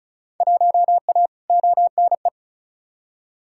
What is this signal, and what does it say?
Morse code, heard as a keyed tone, about 34 wpm, 700 Hz. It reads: 1A ONE
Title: Field Day Exchanges - 35wpm